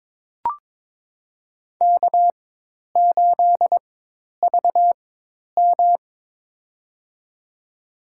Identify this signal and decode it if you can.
Morse code, heard as a keyed tone, about 22 wpm, spaced as for 10 wpm, 700 Hz. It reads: K8VM
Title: USA Call Signs - 4 Characters - 4x Character Spacing - 22wpm